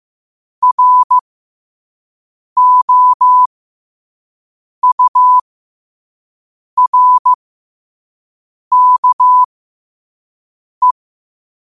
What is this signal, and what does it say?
Morse code, heard as a keyed tone, about 15 wpm, spaced as for 5 wpm, 1000 Hz. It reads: ROURKE
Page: Quotes for Fri, 15 Aug 2025 in Morse Code at 5 words per minute.